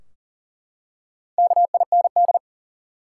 Morse code, heard as a keyed tone, about 40 wpm, 700 Hz. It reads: KIND
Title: Four Letter Words - 40wpm